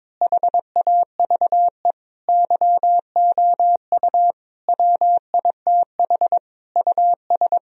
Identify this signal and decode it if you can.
Morse code, heard as a keyed tone, about 21 wpm, 700 Hz. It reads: HAVE YOU WITH US
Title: Speed-Racing - Sets of 5 Words 22wpm